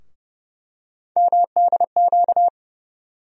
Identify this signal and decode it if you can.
Morse code, heard as a keyed tone, about 30 wpm, 700 Hz. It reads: MDQ
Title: USA Call Sign Postfixes - 3 Letters 30wpm